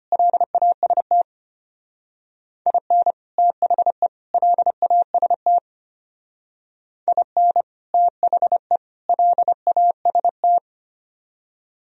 Morse code, heard as a keyed tone, about 31 wpm, 700 Hz. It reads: LAST IN THE LAST IN THE LAST